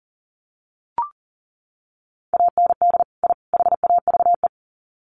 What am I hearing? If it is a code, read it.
Morse code, heard as a keyed tone, about 40 wpm, 700 Hz. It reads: AND I HAVE
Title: Sets of 3 Words 40wpm